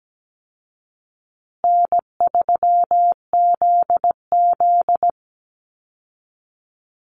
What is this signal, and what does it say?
Morse code, heard as a keyed tone, about 17 wpm, 700 Hz. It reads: N3ZZ
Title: Easy Call Signs Repeated 3x - 17wpm